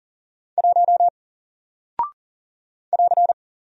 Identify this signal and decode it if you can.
Morse code, heard as a keyed tone, about 40 wpm, 700 Hz. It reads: 1 <AR>
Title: Single Character (Letter:Number:Punctuation:Prosign) Alt 40wpm